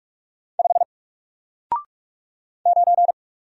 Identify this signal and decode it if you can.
Morse code, heard as a keyed tone, about 44 wpm, 700 Hz. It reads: 5 9
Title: Single Character (Letter:Number:Punctuation:Prosign) Alt 45wpm